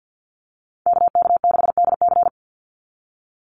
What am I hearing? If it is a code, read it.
Morse code, heard as a keyed tone, about 50 wpm, 700 Hz. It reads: KK6DC
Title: USA Call Signs 50wpm